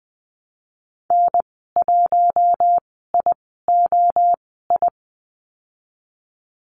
Morse code, heard as a keyed tone, about 20 wpm, spaced as for 15 wpm, 700 Hz. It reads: N1IOI
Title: 048 Lesson 9g - Number 4 - Call Signs 20wpm